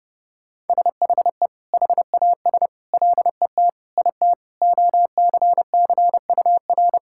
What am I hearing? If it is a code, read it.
Morse code, heard as a keyed tone, about 28 wpm, 700 Hz. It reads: SHE HAS LET IT OCCUR
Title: Sentences from Top 1000 Words - Review Part 6 - 30wpm